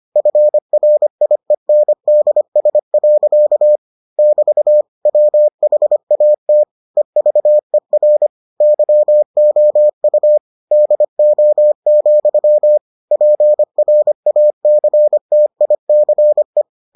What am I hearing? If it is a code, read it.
Morse code, heard as a keyed tone, about 25 wpm, 600 Hz. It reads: FRIENDS. <BT> WHAT EVER YOU DO, PRACTICE